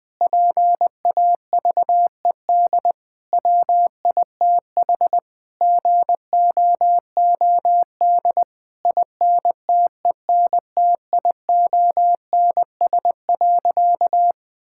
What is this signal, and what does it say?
Morse code, heard as a keyed tone, about 20 wpm, 700 Hz. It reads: PAVED WITH GOOD INTENTIONS.
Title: Sentences with 8-Word Phrases - Spoken Beginning - 20wpm